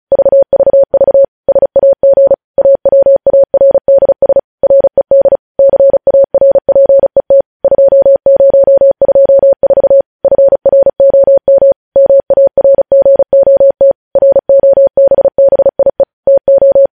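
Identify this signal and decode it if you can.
Morse code, heard as a keyed tone, about 36 wpm, 550 Hz. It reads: VVV SAG AWARDS RED CARPET 2024 FROM MARGOT ROBBIE TO
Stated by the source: Morse code news headlines practice sent at 35 wpm.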